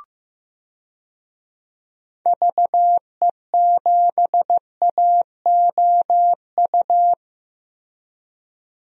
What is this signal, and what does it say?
Morse code, heard as a keyed tone, about 15 wpm, 700 Hz. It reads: VE7AOU
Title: DXCC-100 Call Signs - 6 Characters - Repeated 2x - 15wpm